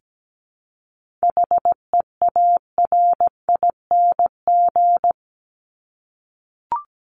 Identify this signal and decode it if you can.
Morse code, heard as a keyed tone, about 17 wpm, 700 Hz. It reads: HEARING
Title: ING Suffix - Top 100 Words with Base Words 17wpm